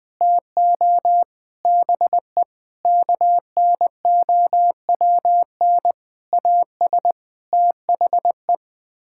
Morse code, heard as a keyed tone, about 21 wpm, 700 Hz. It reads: TO BE KNOWN AS THE
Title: Sets of 8 Words - 20wpm